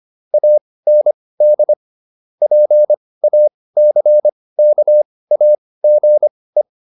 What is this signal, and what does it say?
Morse code, heard as a keyed tone, about 25 wpm, spaced as for 18 wpm, 600 Hz. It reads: AND PACKAGE